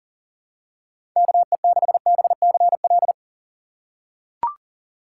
Morse code, heard as a keyed tone, about 40 wpm, 700 Hz. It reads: KE6BCL
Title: USA Call Signs - 6 Characters - 3x Character Spacing - 40wpm